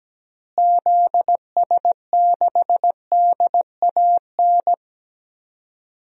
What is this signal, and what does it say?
Morse code, heard as a keyed tone, about 17 wpm, 700 Hz. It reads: ZS6DAN
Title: DXCC-100 Call Signs - 6 Characters - Repeated 3x - 17wpm